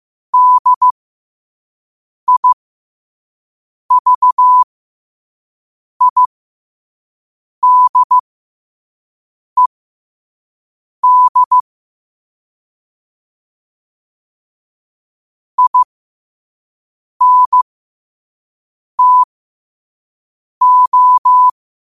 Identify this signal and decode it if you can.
Morse code, heard as a keyed tone, about 15 wpm, spaced as for 5 wpm, 1000 Hz. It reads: DIVIDED INTO
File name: qotdfiles / 5wpm